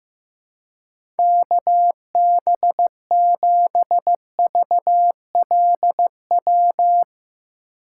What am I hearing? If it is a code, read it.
Morse code, heard as a keyed tone, about 15 wpm, 700 Hz. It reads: KB7VLW
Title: USA Call Signs - 6 Characters - 4x Character Spacing - 15wpm